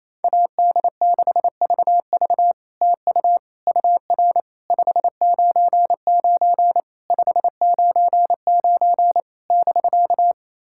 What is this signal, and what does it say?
Morse code, heard as a keyed tone, about 28 wpm, 660 Hz. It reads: AD6VV TU UR 599 599 <BK>
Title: SOTA QSO Practice 28wpm